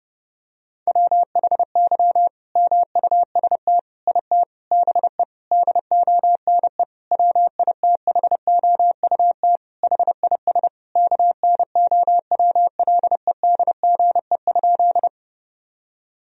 Morse code, heard as a keyed tone, about 32 wpm, 700 Hz. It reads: WHY MUST IT BE DONE WITHOUT HIS KNOWLEDGE?
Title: Sentences from Top 1000 Words - Review Part 3 - 30wpm